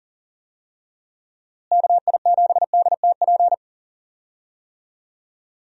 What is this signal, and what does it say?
Morse code, heard as a keyed tone, about 40 wpm, 700 Hz. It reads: KI7DTP
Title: USA Call Signs - 6 Characters - Repeated 3x - 40wpm